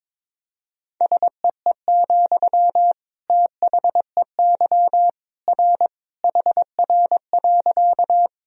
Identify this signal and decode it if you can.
Morse code, heard as a keyed tone, about 22 wpm, 700 Hz. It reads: SEE, THEY R HR.
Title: Sentences from Top 200 Words and Abbreviations 22wpm